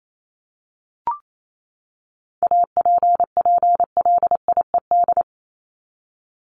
Morse code, heard as a keyed tone, about 28 wpm, 700 Hz. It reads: APPLIED
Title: ED Suffix - Top 100 Words with Base Words 28wpm